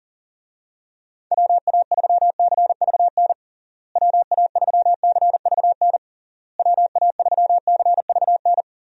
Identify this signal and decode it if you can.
Morse code, heard as a keyed tone, about 40 wpm, 700 Hz. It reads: WA3CVN WA3CVN WA3CVN
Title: USA Call Signs - 6 Characters - Repeated 3x - 40wpm